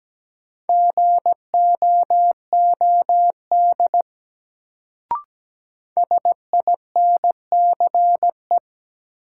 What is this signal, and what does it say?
Morse code, heard as a keyed tone, about 17 wpm, 700 Hz. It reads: GOOD SINCE
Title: Sets of 2 Words - New Word Drill - 17wpm